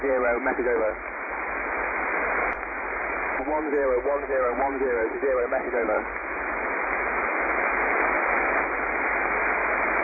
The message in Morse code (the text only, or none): none